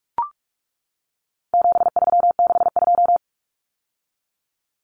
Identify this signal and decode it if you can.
Morse code, heard as a keyed tone, about 44 wpm, 700 Hz. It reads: 7362
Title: Numbers - 4-Digit 45wpm